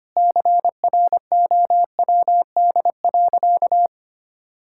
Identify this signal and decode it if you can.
Morse code, heard as a keyed tone, about 25 wpm, 700 Hz. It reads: CROWD.
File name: practice / 25wpm / Sentences with 4-Word Phrases - 25wpm